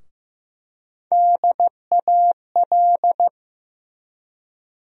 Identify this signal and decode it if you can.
Morse code, heard as a keyed tone, about 15 wpm, 700 Hz. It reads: DAL
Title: USA Call Sign Postfixes - 3 Letters 15wpm